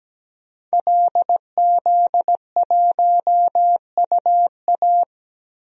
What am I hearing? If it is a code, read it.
Morse code, heard as a keyed tone, about 17 wpm, 700 Hz. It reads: LZ1UA
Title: DXCC-100 Call Signs - 5 Characters - 17wpm